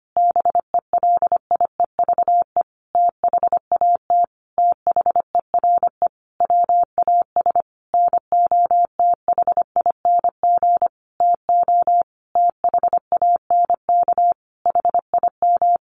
Morse code, heard as a keyed tone, about 25 wpm, 700 Hz. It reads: BELIEVE THAT THERE WAS NOTHING TO THANK HIM
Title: Sentences from Top 1000 Words 25wpm